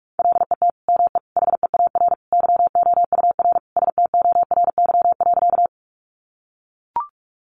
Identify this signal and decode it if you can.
Morse code, heard as a keyed tone, about 44 wpm, 700 Hz. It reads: LET ME HEAR YOUR STORY.
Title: Sentences from Top 300 Words 45wpm